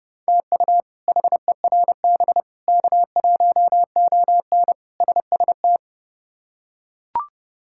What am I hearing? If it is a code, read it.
Morse code, heard as a keyed tone, about 32 wpm, 700 Hz. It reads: TU HERB K1ON SST
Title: K1USN SST QSO Practice - 30wpm